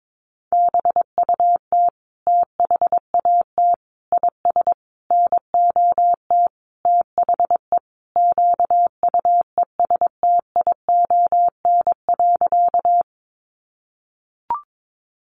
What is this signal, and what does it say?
Morse code, heard as a keyed tone, about 22 wpm, 700 Hz. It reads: BUT THAT IS NOT THE QUESTION.
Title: Sentences from Top 600 Words 22wpm